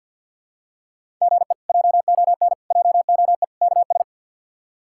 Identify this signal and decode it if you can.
Morse code, heard as a keyed tone, about 47 wpm, 700 Hz. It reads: GE JON JOE KS
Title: K1USN SST QSO Practice - 50wpm